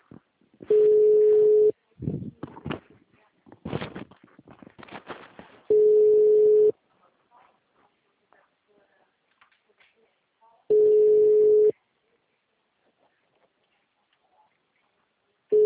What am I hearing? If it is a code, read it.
Ringback tone (Europe, 425 Hz), with speech.